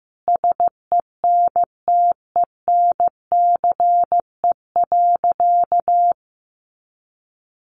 Morse code, heard as a keyed tone, about 15 wpm, 700 Hz. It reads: SENTENCE.
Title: Additional Easy Phrases 15wpm